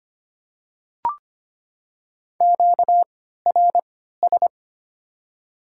Morse code, heard as a keyed tone, about 25 wpm, 700 Hz. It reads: Q R S
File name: Top 90 QSO Elements - 3x Character Spacing - 25wpm